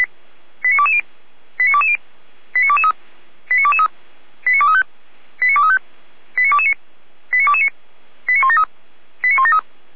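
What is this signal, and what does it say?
Five-tone selective call (ZVEI), heard as quick runs of five tones, tones of about 70 ms. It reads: 89220 89220 89223 89223 89235 89235 89229 89229 89173 89173
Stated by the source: ZVEI five-tone call received on SDR.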